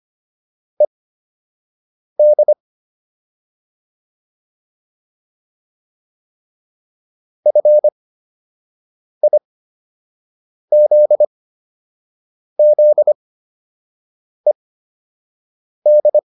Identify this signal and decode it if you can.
Morse code, heard as a keyed tone, about 25 wpm, spaced as for 6 wpm, 600 Hz. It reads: ED FIZZED